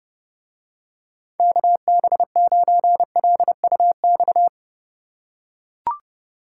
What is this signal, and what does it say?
Morse code, heard as a keyed tone, about 30 wpm, 700 Hz. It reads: KB9LUX